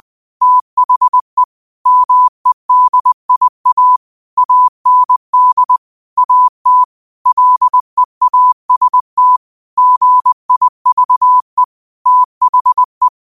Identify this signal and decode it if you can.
Morse code, heard as a keyed tone, about 20 wpm, 1000 Hz. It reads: THE MEDIA AND AT LEAST GIVE THE